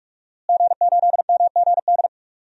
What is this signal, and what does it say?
Morse code, heard as a keyed tone, about 46 wpm, 700 Hz. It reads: G8MGD
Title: DXCC-100 Call Signs - 5 Characters - Repeated 2x - 45wpm